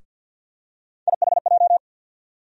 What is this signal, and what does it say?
Morse code, heard as a keyed tone, about 49 wpm, 690 Hz. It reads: IHJ
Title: USA Call Sign Postfixes - 3 Letters 50wpm